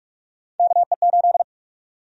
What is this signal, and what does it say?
Morse code, heard as a keyed tone, about 46 wpm, 700 Hz. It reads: KE8
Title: USA Call Sign Prefixes - 45wpm